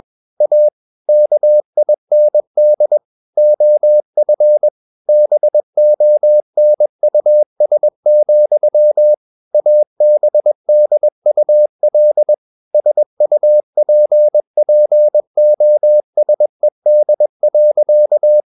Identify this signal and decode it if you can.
Morse code, heard as a keyed tone, about 21 wpm, 600 Hz. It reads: A KIND OF BONUS, ABDUL SUPPOSED.
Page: CW med Gnister